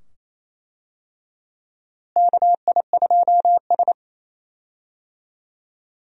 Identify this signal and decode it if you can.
Morse code, heard as a keyed tone, about 28 wpm, 700 Hz. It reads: KI2S
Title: USA Call Signs - 4 Characters - Repeated 2x - 28wpm